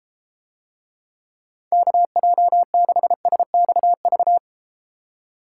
Morse code, heard as a keyed tone, about 32 wpm, 700 Hz. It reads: KJ6SXV